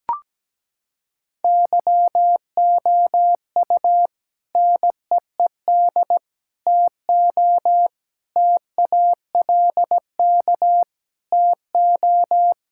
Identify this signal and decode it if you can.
Morse code, heard as a keyed tone, about 17 wpm, 700 Hz. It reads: YOU NEED TO TALK TO
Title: Sentences from Top 1600 Words 17wpm